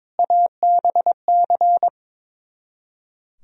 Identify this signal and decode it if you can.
Morse code, heard as a keyed tone, about 22 wpm, 700 Hz. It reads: ABC
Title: Speed-Racing - Sets of 5 Words 22wpm